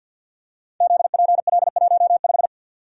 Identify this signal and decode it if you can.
Morse code, heard as a keyed tone, about 50 wpm, 700 Hz. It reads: ZPL15
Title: Code Groups - 5 Letter:Numbers - 50wpm